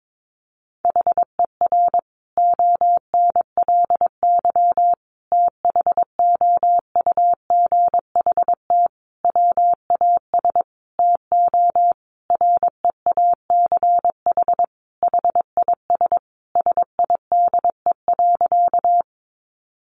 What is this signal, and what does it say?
Morse code, heard as a keyed tone, about 22 wpm, 700 Hz. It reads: HER ONLY THOUGHT WAS TO REACH HIS SIDE.
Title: Sentences from Top 400 Words 22wpm